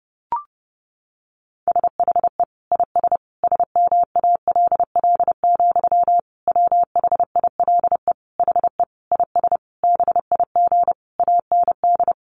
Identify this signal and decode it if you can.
Morse code, heard as a keyed tone, about 30 wpm, 700 Hz. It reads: SHE IS SMALL, WHILE HE IS BIG AND
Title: Sentences from Top 1300 Words 30wpm